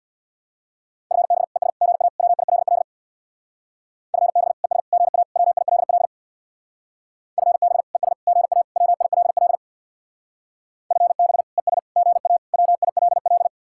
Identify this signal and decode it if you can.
Morse code, heard as a keyed tone, about 62 wpm, 700 Hz. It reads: FB ES GA WILL FB ES GA WILL FB ES GA WILL FB ES GA WILL
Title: Speed-Racing Common QSO Phrases - 50wpm